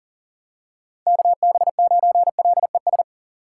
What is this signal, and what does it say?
Morse code, heard as a keyed tone, about 40 wpm, 700 Hz. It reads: KB9LES
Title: USA Call Signs - 6 Characters - 2x Character Spacing - 40wpm